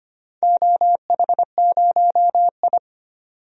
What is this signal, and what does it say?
Morse code, heard as a keyed tone, about 25 wpm, 700 Hz. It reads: OH0I
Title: DXCC-100 Call Signs - 4 Characters - 25wpm